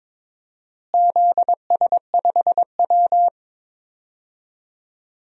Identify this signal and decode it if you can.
Morse code, heard as a keyed tone, about 22 wpm, 700 Hz. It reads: ZS5W
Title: DXCC-100 Call Signs - 4 Characters - Repeated 2x - 22wpm